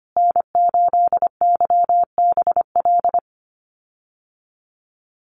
Morse code, heard as a keyed tone, about 25 wpm, 700 Hz. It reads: N8YBL
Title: USA Call Signs Repeated 3x 25wpm